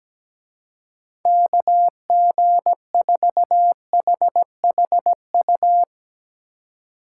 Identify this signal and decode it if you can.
Morse code, heard as a keyed tone, about 18 wpm, 700 Hz. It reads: KG4HHU
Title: Speed Racing - USA 2x3 Call Signs 17wpm